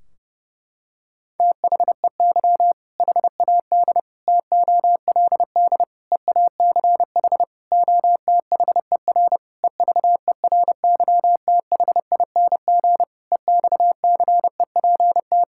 Morse code, heard as a keyed tone, about 30 wpm, 700 Hz. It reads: THEY HAD TOLD EACH OTHER EVERYTHING EXCEPT
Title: Sentences from Top 1600 Words 30wpm